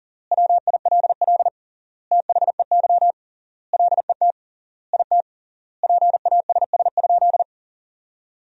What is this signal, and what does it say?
Morse code, heard as a keyed tone, about 40 wpm, 700 Hz. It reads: WILL THEY LET IT PASS?
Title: 3x Word-Spacing - Sentences from Top 400 Words - 40wpm